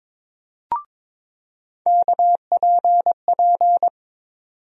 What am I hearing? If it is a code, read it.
Morse code, heard as a keyed tone, about 22 wpm, 700 Hz. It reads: KPP
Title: USA Call Sign Postfixes - 3 Letters 22wpm